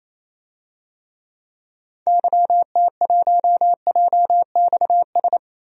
Morse code, heard as a keyed tone, about 28 wpm, 700 Hz. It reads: YT1JXS
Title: DXCC-100 Call Signs - 6 Characters - Repeated 2x - 28wpm